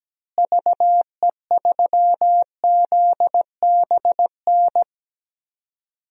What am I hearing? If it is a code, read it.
Morse code, heard as a keyed tone, about 17 wpm, 700 Hz. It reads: VE3ZBN
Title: DXCC-100 Call Signs - 6 Characters - Repeated 2x - 17wpm